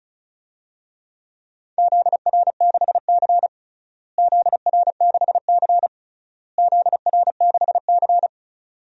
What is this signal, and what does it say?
Morse code, heard as a keyed tone, about 36 wpm, 700 Hz. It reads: ZR6C ZR6C ZR6C
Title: DXCC-100 Call Signs - 4 Characters - Repeated 3x - 35wpm